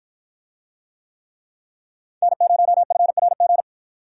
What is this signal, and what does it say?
Morse code, heard as a keyed tone, about 53 wpm, 700 Hz. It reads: N0FRG